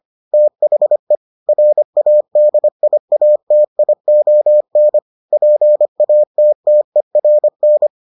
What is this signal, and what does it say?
Morse code, heard as a keyed tone, about 25 wpm, 600 Hz. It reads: THE RADIATION PATTERN